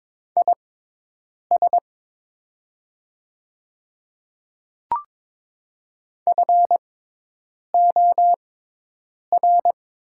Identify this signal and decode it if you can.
Morse code, heard as a keyed tone, about 22 wpm, spaced as for 8 wpm, 700 Hz. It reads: IS FOR
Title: 6x Character Spacing - Top 200 Words - 22wpm